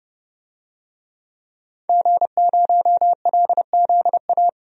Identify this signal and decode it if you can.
Morse code, heard as a keyed tone, about 30 wpm, 700 Hz. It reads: G0LZA
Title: DXCC-100 Call Signs - 5 Characters - Repeated 2x - 30wpm